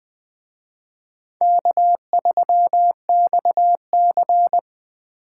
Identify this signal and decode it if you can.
Morse code, heard as a keyed tone, about 24 wpm, 700 Hz. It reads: K3XC
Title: Speed Racing - USA 1x2 Call Signs 20wpm